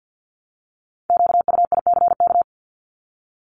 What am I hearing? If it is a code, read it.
Morse code, heard as a keyed tone, about 48 wpm, 700 Hz. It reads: QUICK
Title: LY Suffix - Top 100 Words with Base Words 50wpm